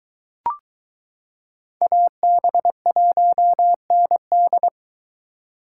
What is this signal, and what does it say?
Morse code, heard as a keyed tone, about 22 wpm, 700 Hz. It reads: AB1ND